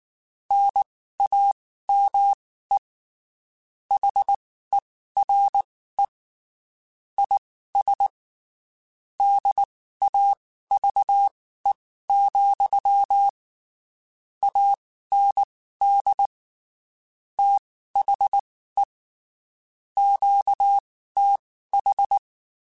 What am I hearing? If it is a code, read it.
Morse code, heard as a keyed tone, about 19 wpm, spaced as for 14 wpm, 800 Hz. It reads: NAME HERE IS DAVE, AND THE QTH
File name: (spaced) / QSOs Spaced-mp3s / kc7sac @18 sp